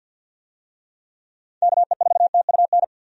Morse code, heard as a keyed tone, about 50 wpm, 700 Hz. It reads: KE4TUN
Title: USA Call Signs - 6 Characters - 50wpm